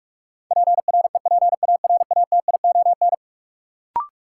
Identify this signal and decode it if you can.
Morse code, heard as a keyed tone, about 46 wpm, 700 Hz. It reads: PREPARATION
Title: Sentences from Top 1400 Words - New Word Drill Alt - 45wpm